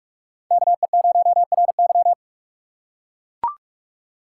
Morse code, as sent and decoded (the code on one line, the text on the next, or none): -.- . ----- .-. -.--
KE0RY